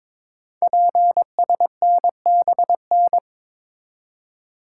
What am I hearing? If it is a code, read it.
Morse code, heard as a keyed tone, about 22 wpm, 700 Hz. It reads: PSNBN
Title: Code Groups - 5 Letters - 22wpm